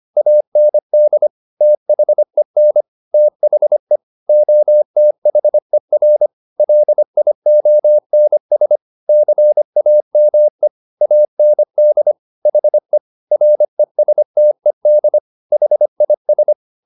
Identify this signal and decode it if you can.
Morse code, heard as a keyed tone, about 25 wpm, 600 Hz. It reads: AND THEN THE OTHER LIONS CAME AND HE RESTED HIS